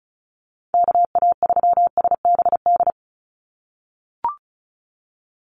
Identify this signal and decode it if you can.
Morse code, heard as a keyed tone, about 34 wpm, 700 Hz. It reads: KA3SBD
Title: USA Call Signs 35wpm